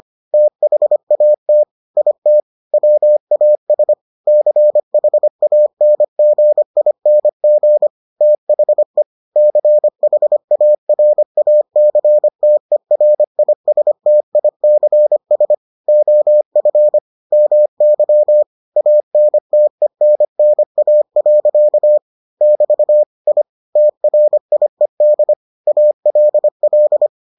Morse code, heard as a keyed tone, about 25 wpm, 600 Hz. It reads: THAT IT WAS CHANGING THE CHARACTERISTICS OF MY ANTENNA. <BT> I TRIED ALL